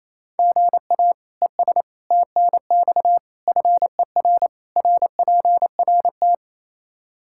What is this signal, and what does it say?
Morse code, heard as a keyed tone, about 28 wpm, 700 Hz. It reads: GA ES TNX FER RPRT
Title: K1USN SST QSO Practice 28wpm